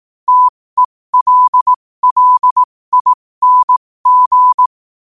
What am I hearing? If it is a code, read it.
Morse code, heard as a keyed tone, about 18 wpm, spaced as for 15 wpm, 1000 Hz. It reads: TELLING
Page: New quotes every day in morse code at 15 Words per minute.